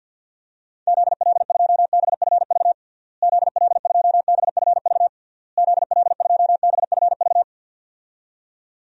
Morse code, heard as a keyed tone, about 50 wpm, 700 Hz. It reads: ZL2BFV ZL2BFV ZL2BFV
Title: DXCC-100 Call Signs - 6 Characters - Repeated 3x - 50wpm